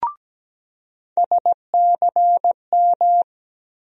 Morse code, heard as a keyed tone, about 17 wpm, 700 Hz. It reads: SCM
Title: USA Call Sign Postfixes - 3 Letters 17wpm